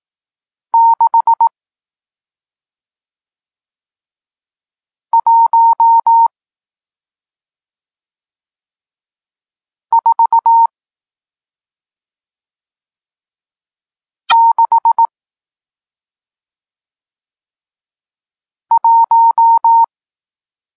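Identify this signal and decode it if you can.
Morse code, heard as a keyed tone, about 18 wpm, spaced as for 2 wpm, 920 Hz. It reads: 61461
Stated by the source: Morse code lesson teaching digits 1, 4 and 6, from SP9MOA, Niepołomicki Klub Krótkofalowców (Poland) amateur radio club.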